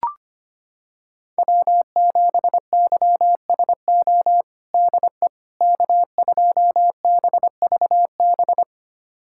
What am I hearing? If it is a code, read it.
Morse code, heard as a keyed tone, about 25 wpm, 700 Hz. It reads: W7YSO DE K2BVB
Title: USA Callsign DE Callsign - 5 Characters - 25wpm